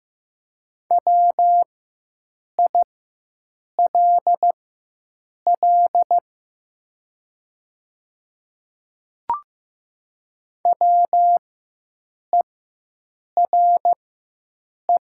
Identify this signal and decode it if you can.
Morse code, heard as a keyed tone, about 15 wpm, spaced as for 7 wpm, 700 Hz. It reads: WILL WERE
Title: 4x Character Spacing - Top 100 Words - 15wpm